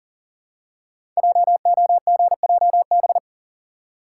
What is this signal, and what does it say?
Morse code, heard as a keyed tone, about 40 wpm, 700 Hz. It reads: JOGJB
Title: Code Groups - 5 Letters - 40wpm